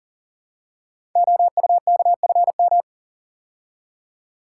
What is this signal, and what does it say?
Morse code, heard as a keyed tone, about 39 wpm, 700 Hz. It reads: OUKFM
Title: Code Groups - 5 Letters - 40wpm